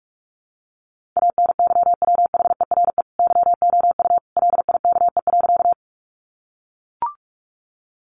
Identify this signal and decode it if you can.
Morse code, heard as a keyed tone, about 44 wpm, 700 Hz. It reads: ANYWHERE YOU LIKE.
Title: Sentences from Top 1400 Words 45wpm